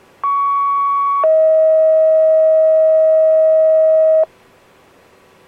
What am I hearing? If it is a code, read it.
Two-tone page, heard as a one-second tone, then a three-second tone, about 1120 then 640 Hz. It is heard from 0.2 to 4.2 s.